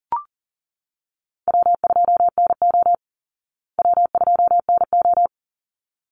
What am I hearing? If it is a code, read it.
Morse code, heard as a keyed tone, about 40 wpm, 700 Hz. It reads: W2NO W2NO
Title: Easy Call Signs Repeated 2x - 40wpm